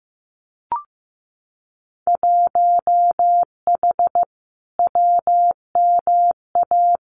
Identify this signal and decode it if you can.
Morse code, heard as a keyed tone, about 15 wpm, 700 Hz. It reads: 1H WMA
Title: Winter Field Day Exchanges 15wpm